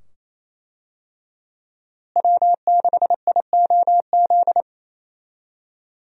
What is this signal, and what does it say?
Morse code, heard as a keyed tone, about 28 wpm, 700 Hz. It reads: W6IOZ
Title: USA Call Signs - 5 Characters - Repeated 2x - 28wpm